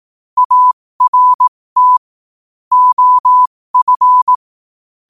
Morse code, heard as a keyed tone, about 18 wpm, spaced as for 15 wpm, 1000 Hz. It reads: ART OF
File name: qotdfiles / 15wpm